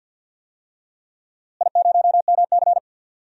Morse code, heard as a keyed tone, about 48 wpm, 700 Hz. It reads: I0MC
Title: DXCC-100 Call Signs - 4 Characters - Repeated 3x - 50wpm